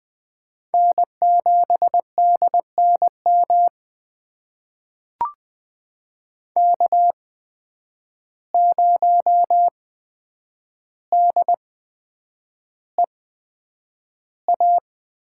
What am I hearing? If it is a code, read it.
Morse code, heard as a keyed tone, about 20 wpm, spaced as for 9 wpm, 700 Hz. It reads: N7DNM K0DEA
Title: USA Call Signs - 5 Characters - 8x Character Spacing - 20wpm